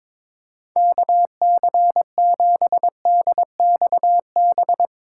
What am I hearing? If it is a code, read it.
Morse code, heard as a keyed tone, about 22 wpm, 700 Hz. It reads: KC7DXB
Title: USA Call Signs - 6 Characters - 3x Character Spacing - 22wpm